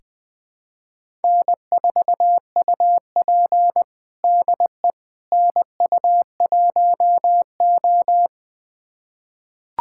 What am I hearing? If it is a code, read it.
Morse code, heard as a keyed tone, about 20 wpm, 700 Hz. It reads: N4UP DE NU1O
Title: USA Callsign DE Callsign - 4 Characters - 20wpm